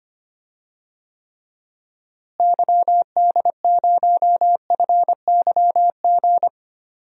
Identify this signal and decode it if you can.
Morse code, heard as a keyed tone, about 25 wpm, 700 Hz. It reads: YD0FYG
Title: DXCC-100 Call Signs - 6 Characters - Repeated 2x - 25wpm